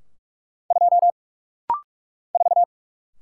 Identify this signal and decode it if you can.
Morse code, heard as a keyed tone, about 46 wpm, 700 Hz. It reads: 2 4
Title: Single Letter-Number - Rapid-Fire 45wpm